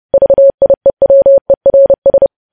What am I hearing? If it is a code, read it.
Morse code, heard as a keyed tone, about 30 wpm, 550 Hz. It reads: VIEWERS
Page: news headlines in morse code at 30 wpm